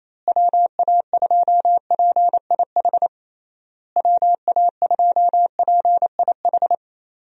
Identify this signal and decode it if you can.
Morse code, heard as a keyed tone, about 28 wpm, 700 Hz. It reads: WA2PIH WA2PIH
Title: USA Call Signs - 6 Characters - Repeated 3x - 28wpm